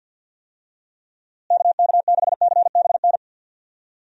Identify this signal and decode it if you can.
Morse code, heard as a keyed tone, about 52 wpm, 700 Hz. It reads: KK6CBN
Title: USA Call Signs - 6 Characters - Repeated 2x - 50wpm